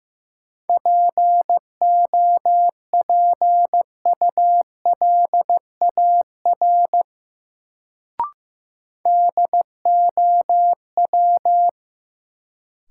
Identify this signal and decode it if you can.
Morse code, heard as a keyed tone, about 16 wpm, 700 Hz. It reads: POPULAR DOW
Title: Sets of 5 Words - New Word Drill - 15wpm